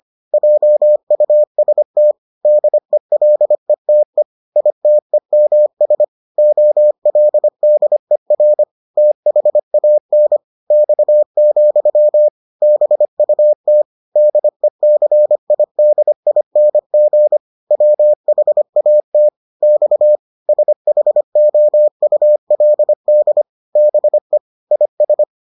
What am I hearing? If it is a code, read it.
Morse code, heard as a keyed tone, about 25 wpm, 600 Hz. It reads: JUST DELETE ITEMS OLDER THAN X, BUT DECIDING WHAT X SHOULD BE IS